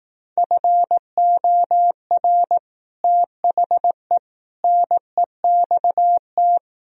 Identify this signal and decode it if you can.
Morse code, heard as a keyed tone, about 18 wpm, 700 Hz. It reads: FOR THE NEXT
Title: Speed-Racing - Sets of 3 Words 15wpm